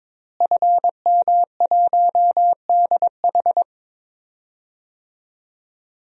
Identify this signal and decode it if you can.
Morse code, heard as a keyed tone, about 22 wpm, 700 Hz. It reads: FM1DH
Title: DXCC-100 Call Signs - 5 Characters - Repeated 2x - 22wpm